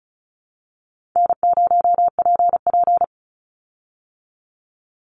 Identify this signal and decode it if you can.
Morse code, heard as a keyed tone, about 34 wpm, 700 Hz. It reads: N0PP
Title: Call Signs Repeated 2x - 35wpm